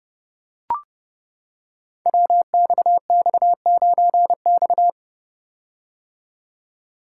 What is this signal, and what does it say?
Morse code, heard as a keyed tone, about 28 wpm, 700 Hz. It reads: WXX9X
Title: Code Groups - 5 Letter:Numbers - 30wpm